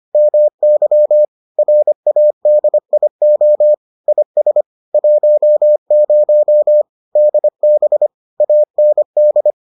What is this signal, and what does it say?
Morse code, heard as a keyed tone, about 25 wpm, 600 Hz. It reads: MY RADIO IS 10 DB AND